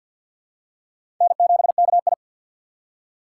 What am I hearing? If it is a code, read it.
Morse code, heard as a keyed tone, about 49 wpm, 700 Hz. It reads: N7KI
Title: USA Call Signs - 4 Characters - 2x Character Spacing - 50wpm